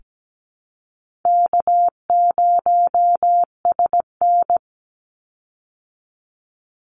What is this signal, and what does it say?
Morse code, heard as a keyed tone, about 17 wpm, 700 Hz. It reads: K0SN
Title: Call Signs Repeated 3x - 17wpm